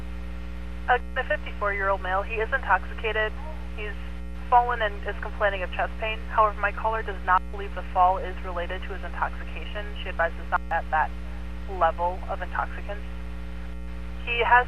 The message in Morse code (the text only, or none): none